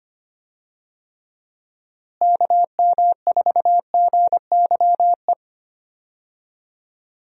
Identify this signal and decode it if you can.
Morse code, heard as a keyed tone, about 25 wpm, 700 Hz. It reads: KM4GYE
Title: USA Call Signs - 6 Characters - Repeated 3x - 25wpm